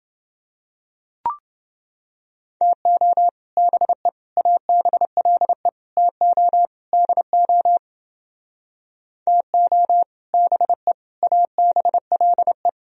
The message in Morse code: TO BE ABLE TO DO TO BE ABLE

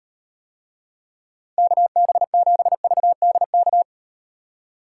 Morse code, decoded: KB7VDK

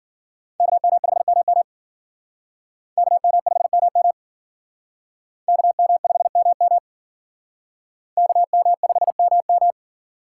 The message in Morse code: KM5MM KM5MM KM5MM KM5MM

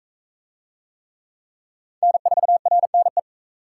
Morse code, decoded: N4RNE

